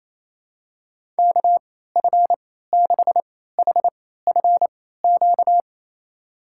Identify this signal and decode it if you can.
Morse code, heard as a keyed tone, about 28 wpm, spaced as for 16 wpm, 700 Hz. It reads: KF6HFQ
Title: USA Call Signs - 6 Characters - 3x Character Spacing - 28wpm